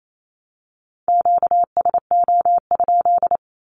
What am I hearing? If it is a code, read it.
Morse code, heard as a keyed tone, about 28 wpm, 700 Hz. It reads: QSO?
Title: Q-Codes 28wpm